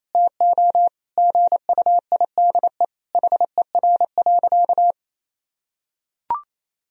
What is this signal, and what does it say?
Morse code, heard as a keyed tone, about 28 wpm, 700 Hz. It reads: TO GUIDE HER.
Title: Sentences from Top 1000 Words - Review Part 4 - 28wpm